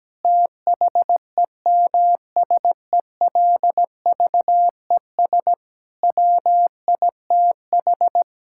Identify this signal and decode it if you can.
Morse code, heard as a keyed tone, about 17 wpm, 700 Hz. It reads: THEMSELVES WITH
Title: Sentences from Top 1000 Words - Review Part 5 - 17wpm